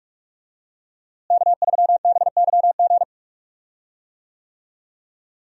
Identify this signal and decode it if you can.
Morse code, heard as a keyed tone, about 41 wpm, 700 Hz. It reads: K3BYG
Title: USA Call Signs - 5 Characters - Repeated 3x - 45wpm